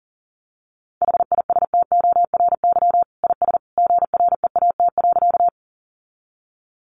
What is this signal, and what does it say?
Morse code, heard as a keyed tone, about 40 wpm, 700 Hz. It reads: HISTORY IS GREAT.
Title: Sentences from Top 600 Words 40wpm